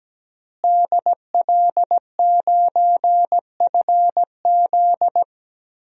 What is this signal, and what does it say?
Morse code, heard as a keyed tone, about 17 wpm, 700 Hz. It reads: DL9FZ
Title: DXCC-100 Call Signs - 5 Characters - Repeated 3x - 17wpm